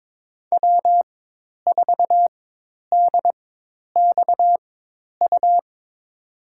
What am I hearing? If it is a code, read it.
Morse code, heard as a keyed tone, about 22 wpm, spaced as for 10 wpm, 700 Hz. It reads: W4DXU
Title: USA Call Signs - 5 Characters - 4x Character Spacing - 22wpm